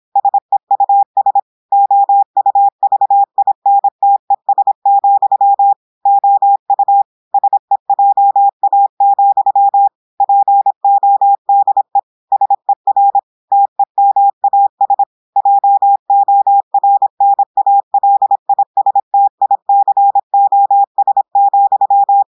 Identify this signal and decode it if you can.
Morse code, heard as a keyed tone, about 26 wpm, 800 Hz. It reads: SEUS OUVINTES, OU SEJA, PODE SER TEMAS JORNALISTICOS,
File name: portugais texte en clair 2 vitesse 26 mots minutes